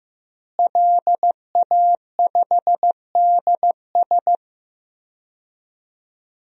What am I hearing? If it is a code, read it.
Morse code, heard as a keyed tone, about 15 wpm, 700 Hz. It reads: LA5DS
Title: DXCC-100 Call Signs - 5 Characters - Repeated 3x - 15wpm